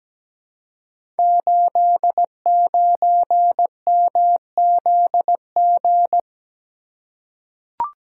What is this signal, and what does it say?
Morse code, heard as a keyed tone, about 17 wpm, 700 Hz. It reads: 89MZG
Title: Code Groups - 5 Letter:Numbers - 17wpm